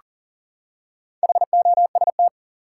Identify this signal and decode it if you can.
Morse code, heard as a keyed tone, about 40 wpm, 700 Hz. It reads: HOST
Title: Four Letter Words - 40wpm